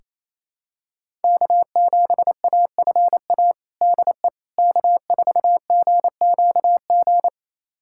Morse code, heard as a keyed tone, about 28 wpm, 700 Hz. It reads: K7AFA DE K4GQG
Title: USA Callsign DE Callsign - 5 Characters - 28wpm